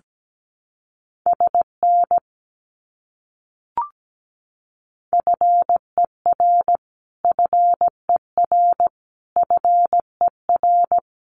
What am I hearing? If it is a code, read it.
Morse code, heard as a keyed tone, about 17 wpm, 700 Hz. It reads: SN FER FER FER
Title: Abbreviations Repeated 3x - 17wpm